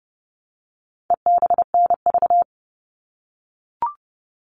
Morse code, heard as a keyed tone, about 28 wpm, 700 Hz. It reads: EBNV
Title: All Cut Numbers - 4-Digit 30wpm